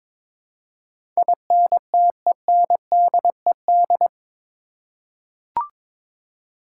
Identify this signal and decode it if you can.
Morse code, heard as a keyed tone, about 21 wpm, 700 Hz. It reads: INTENDED
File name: practice / 22wpm / Eight Letter Words - 22wpm